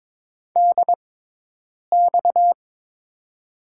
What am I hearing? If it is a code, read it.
Morse code, heard as a keyed tone, about 21 wpm, spaced as for 8 wpm, 700 Hz. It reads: DX
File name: Top 90 QSO Elements - 6x Character Spacing - 22wpm